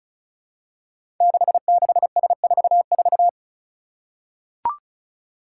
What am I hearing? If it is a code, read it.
Morse code, heard as a keyed tone, about 36 wpm, 700 Hz. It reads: 66S44
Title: Five Challenging Letter-Number Combinations 35wpm